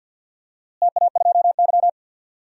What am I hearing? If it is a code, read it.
Morse code, heard as a keyed tone, about 46 wpm, 700 Hz. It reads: TA2Y
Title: DXCC-100 Call Signs - 4 Characters - Repeated 2x - 50wpm